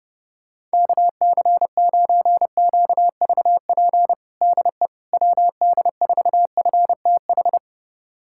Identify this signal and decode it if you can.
Morse code, heard as a keyed tone, about 30 wpm, 700 Hz. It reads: KC9QVP DE WD4FTH
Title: USA Callsign DE Callsign - 6 Characters - 30wpm